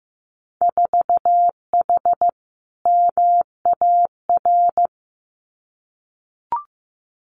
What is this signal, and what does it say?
Morse code, heard as a keyed tone, about 15 wpm, 700 Hz. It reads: 4H MAR
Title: Winter Field Day Exchanges Encore 15wpm